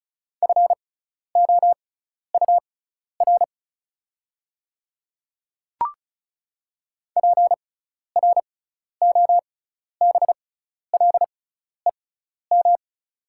Morse code, heard as a keyed tone, about 34 wpm, spaced as for 12 wpm, 690 Hz. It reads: FOUR PROBLEM
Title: 6x Character Spacing - Top 500 Words - 35wpm